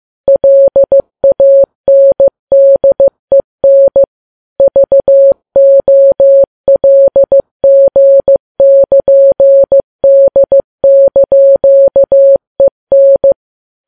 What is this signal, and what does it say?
Morse code, heard as a keyed tone, about 15 wpm, 550 Hz. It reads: LANDEN VOLG<KN>D)EN
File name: Files Web tls / Morse on9fv / morse15kps